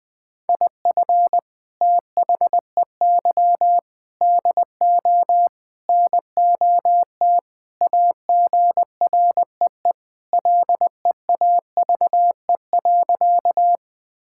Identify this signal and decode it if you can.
Morse code, heard as a keyed tone, about 21 wpm, 700 Hz. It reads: IF THEY DO NOT AGREE LEAVE.
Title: Sentences with 4-Word Phrases - 20wpm